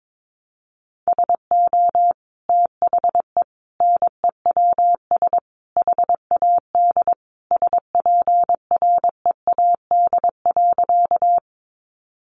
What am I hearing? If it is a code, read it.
Morse code, heard as a keyed tone, about 22 wpm, 700 Hz. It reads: SO THE NEWS HAD SPREAD.
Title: Sentences from Top 1200 Words 22wpm